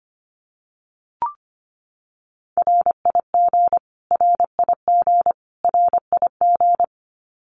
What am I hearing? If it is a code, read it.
Morse code, heard as a keyed tone, about 25 wpm, 700 Hz. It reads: RIG RIG RIG
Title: Abbreviations Repeated 3x - 25wpm